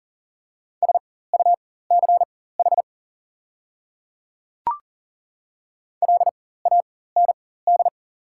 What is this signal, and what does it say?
Morse code, heard as a keyed tone, about 37 wpm, spaced as for 18 wpm, 690 Hz. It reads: SUCH LAND
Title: 4x Character Spacing - Top 200 Words - 40wpm